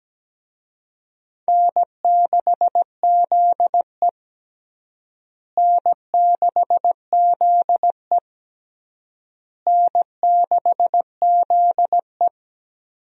Morse code, heard as a keyed tone, about 17 wpm, 700 Hz. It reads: N6ZE N6ZE N6ZE
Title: USA Call Signs - 4 Characters - Repeated 3x - 17wpm